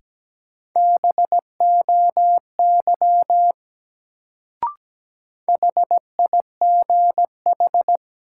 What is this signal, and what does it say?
Morse code, heard as a keyed tone, about 17 wpm, 700 Hz. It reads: BOY HIGH
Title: Sentences from Top 200 Words - New Word Drill - 17wpm